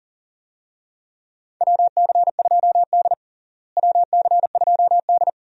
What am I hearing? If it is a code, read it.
Morse code, heard as a keyed tone, about 40 wpm, 700 Hz. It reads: WC2D WC2D
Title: USA Call Signs - 4 Characters - Repeated 3x - 40wpm